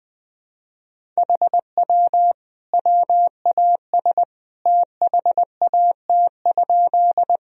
Morse code, heard as a keyed tone, about 21 wpm, 700 Hz. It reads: HW WAS THAT?
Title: Sentences from Top 100 Words and Abbreviations 20wpm